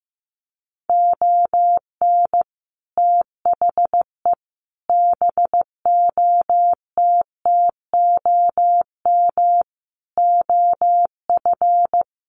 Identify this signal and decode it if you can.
Morse code, heard as a keyed tone, about 15 wpm, 700 Hz. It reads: ON THE BOTTOM OF
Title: Sets of 5 Words Encore2 15wpm